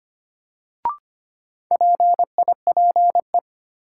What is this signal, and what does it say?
Morse code, heard as a keyed tone, about 25 wpm, 700 Hz. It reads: PIPE
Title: Sentences from Top 1500 Words - New Word Drill - 25wpm